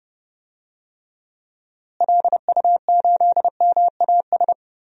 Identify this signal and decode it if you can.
Morse code, heard as a keyed tone, about 30 wpm, 700 Hz. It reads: LU8MAS
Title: DXCC-100 Call Signs - 6 Characters - Repeated 2x - 30wpm